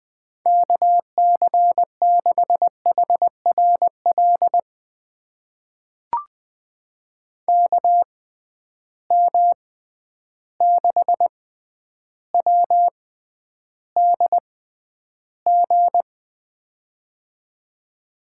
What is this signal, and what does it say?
Morse code, heard as a keyed tone, about 20 wpm, spaced as for 10 wpm, 700 Hz. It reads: KC6HRL KM6WDG
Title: USA Call Signs - 6 Characters - 6x Character Spacing - 20wpm